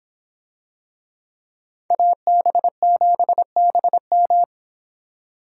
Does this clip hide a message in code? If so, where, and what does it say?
Yes: Morse code from 1.9 to 4.4 s, heard as a keyed tone, about 26 wpm, 700 Hz. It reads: AB7BM